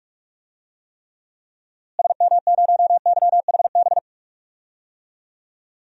Morse code, heard as a keyed tone, about 41 wpm, 700 Hz. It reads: SM0YHB
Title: DXCC-100 Call Signs - 6 Characters - Repeated 2x - 45wpm